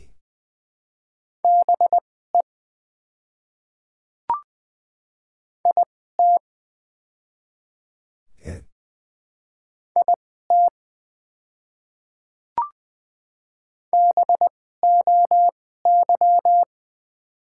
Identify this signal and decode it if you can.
Morse code, heard as a keyed tone, about 20 wpm, spaced as for 14 wpm, 700 Hz. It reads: BE IT IT BOY
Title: 2x Character Spacing - Top 300 Words - 20wpm